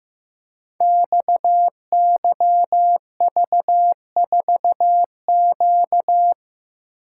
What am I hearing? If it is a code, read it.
Morse code, heard as a keyed tone, about 15 wpm, 700 Hz. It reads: XYV4Q
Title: Code Groups - 5 Letter:Numbers - 15wpm